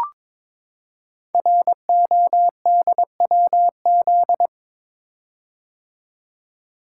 Morse code, heard as a keyed tone, about 22 wpm, 700 Hz. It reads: RODWZ
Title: Code Groups - 5 Letters - 22wpm